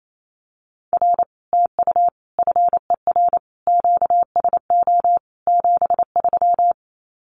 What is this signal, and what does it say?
Morse code, heard as a keyed tone, about 28 wpm, 700 Hz. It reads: R TU FER QSO 73
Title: Common QSO Phrases Encore 28wpm